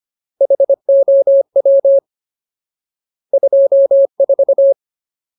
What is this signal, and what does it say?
Morse code, heard as a keyed tone, about 25 wpm, 550 Hz. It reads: HOW 24
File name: Ultimate Word List / 25WPM Ultimate